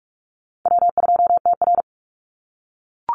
Morse code, heard as a keyed tone, about 44 wpm, 700 Hz. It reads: W2TR
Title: Call Signs - 45wpm